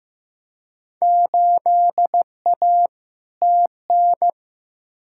Morse code, heard as a keyed tone, about 15 wpm, 700 Hz. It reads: 8A TN
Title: Field Day Exchanges Encore - 15wpm